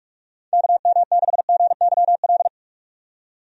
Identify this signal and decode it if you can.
Morse code, heard as a keyed tone, about 46 wpm, 700 Hz. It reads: KM6GYL
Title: USA Call Signs - 6 Characters - 45wpm